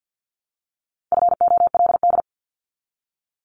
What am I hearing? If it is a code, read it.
Morse code, heard as a keyed tone, about 50 wpm, 700 Hz. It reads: FOLD
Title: Multi-letter Phonemes Repeated 3x 50wpm